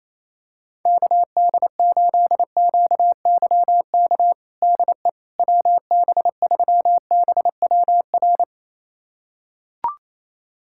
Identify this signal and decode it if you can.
Morse code, heard as a keyed tone, about 28 wpm, 700 Hz. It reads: KD8QYK DE WB3BWR
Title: USA Callsign DE Callsign 28wpm